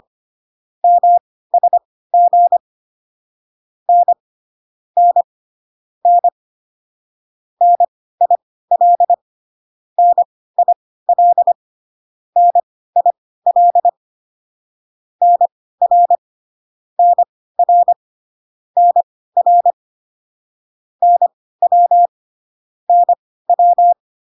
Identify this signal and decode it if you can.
Morse code, heard as a keyed tone, about 25 wpm, spaced as for 16 wpm, 700 Hz. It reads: MSG N N N NIL NIL NIL NR NR NR NW NW